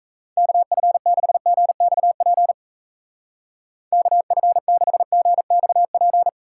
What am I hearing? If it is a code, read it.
Morse code, heard as a keyed tone, about 37 wpm, 700 Hz. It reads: KF6GXP KF6GXP